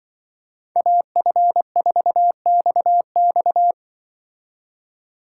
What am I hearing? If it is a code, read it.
Morse code, heard as a keyed tone, about 23 wpm, 700 Hz. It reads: AF4XX
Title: Speed Racing - USA 2x2 Call Signs 20wpm